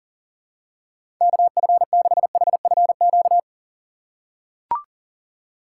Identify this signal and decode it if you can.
Morse code, heard as a keyed tone, about 40 wpm, 700 Hz. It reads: KF6HFQ
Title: USA Call Signs - 4x Character Spacing - 40wpm